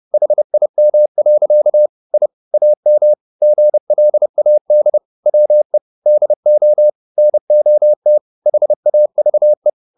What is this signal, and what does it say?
Morse code, heard as a keyed tone, about 30 wpm, 600 Hz. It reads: HIM. I AM GLAD WE DO NOT HAVE